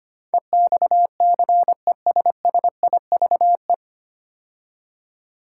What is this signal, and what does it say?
Morse code, heard as a keyed tone, about 25 wpm, 700 Hz. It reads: EXCESSIVE
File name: practice / 25wpm / 9 Letter Words - First 3 Letters Given - 25wpm